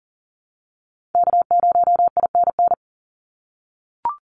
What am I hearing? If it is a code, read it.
Morse code, heard as a keyed tone, about 40 wpm, 700 Hz. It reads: K0INN
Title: Call Signs - 40wpm